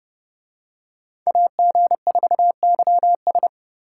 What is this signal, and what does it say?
Morse code, heard as a keyed tone, about 30 wpm, 700 Hz. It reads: AG4YS